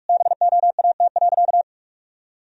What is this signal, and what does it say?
Morse code, heard as a keyed tone, about 41 wpm, 700 Hz. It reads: BOAT.
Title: Sentences from Top 1000 Words - Review Part 3 - 45wpm